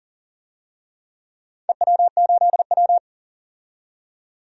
Morse code, heard as a keyed tone, about 40 wpm, 700 Hz. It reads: EW8W
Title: DXCC-100 Call Signs - 4 Characters - 40wpm